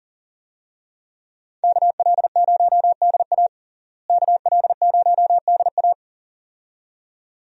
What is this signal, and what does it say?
Morse code, heard as a keyed tone, about 40 wpm, 700 Hz. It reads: KL0DA KL0DA
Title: USA Call Signs - 5 Characters - Repeated 2x - 40wpm